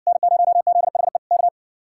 Morse code, heard as a keyed tone, about 70 wpm, 700 Hz. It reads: N0ZHE K